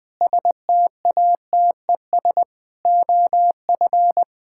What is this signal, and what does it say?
Morse code, heard as a keyed tone, about 21 wpm, 700 Hz. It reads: STATES OF
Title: Sentences with 5-Word Phrases - 20wpm